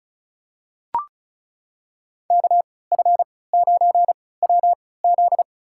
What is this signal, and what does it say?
Morse code, heard as a keyed tone, about 36 wpm, spaced as for 20 wpm, 700 Hz. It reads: KF9WZ
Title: USA Call Signs - 3x Character Spacing - 35wpm